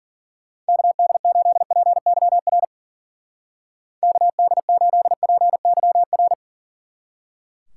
Morse code, heard as a keyed tone, about 50 wpm, 700 Hz. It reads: KD8PYR KD8PYR
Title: Speed Racing - USA 2x3 Call Signs 40wpm